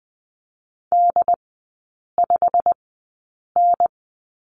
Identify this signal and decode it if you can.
Morse code, heard as a keyed tone, about 20 wpm, spaced as for 8 wpm, 700 Hz. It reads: D5N
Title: 093 Lesson 16e - Letter c - Three Characters 20wpm